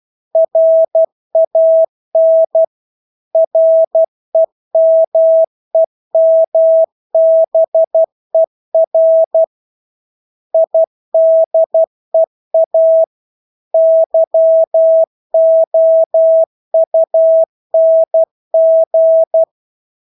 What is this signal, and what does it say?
Morse code, heard as a keyed tone, about 12 wpm, 650 Hz. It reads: RAN REMEMBER IDEA YOUNG
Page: Engelske ord 12 wpm